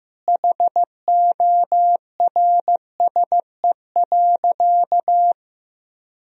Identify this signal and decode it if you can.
Morse code, heard as a keyed tone, about 15 wpm, 700 Hz. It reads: HORSE.
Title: Sentences from Top 1000 Words - Review Part 4 - 15wpm